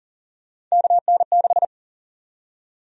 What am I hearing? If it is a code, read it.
Morse code, heard as a keyed tone, about 40 wpm, 700 Hz. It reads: KN6
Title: USA Call Sign Prefixes - Repeated 2x - 40wpm